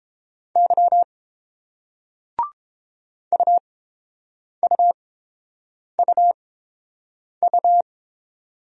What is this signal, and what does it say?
Morse code, heard as a keyed tone, about 30 wpm, spaced as for 7 wpm, 700 Hz. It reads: Y UUUU